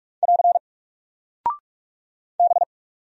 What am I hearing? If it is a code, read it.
Morse code, heard as a keyed tone, about 44 wpm, 700 Hz. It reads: <AR> B
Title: Single Character (Letter:Number:Punctuation:Prosign) 45wpm